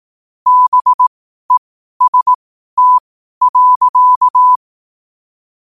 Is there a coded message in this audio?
Yes: Morse code from 0.5 to 4.6 s, heard as a keyed tone, about 18 wpm, spaced as for 12 wpm, 1000 Hz. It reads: BEST.